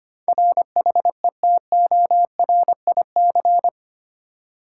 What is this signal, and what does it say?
Morse code, heard as a keyed tone, about 25 wpm, 700 Hz. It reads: RHETORIC
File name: Eight Letter Words - 25wpm